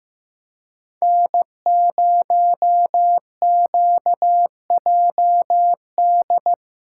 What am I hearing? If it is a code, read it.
Morse code, heard as a keyed tone, about 15 wpm, 700 Hz. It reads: N0QJD
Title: USA Callsign DE Callsign - 5 Characters - 15wpm